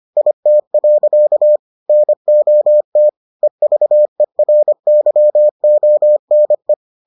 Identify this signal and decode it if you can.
Morse code, heard as a keyed tone, about 25 wpm, 600 Hz. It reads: IT. NOT EVERYONE